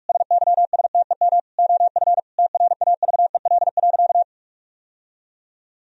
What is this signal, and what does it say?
Morse code, heard as a keyed tone, about 46 wpm, 700 Hz. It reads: SYSTEM OF TRAVEL.
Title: Sentences from Top 1000 Words - Review Part 4 - 45wpm